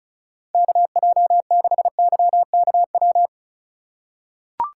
Morse code, heard as a keyed tone, about 36 wpm, 700 Hz. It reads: KJ6YKW
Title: USA Call Signs - 6 Characters - 3x Character Spacing - 35wpm